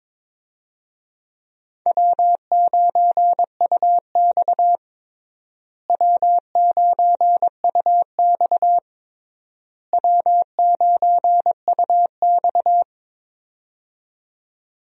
Morse code, heard as a keyed tone, about 22 wpm, 700 Hz. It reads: W9UX W9UX W9UX
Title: USA Call Signs - 4 Characters - Repeated 3x - 22wpm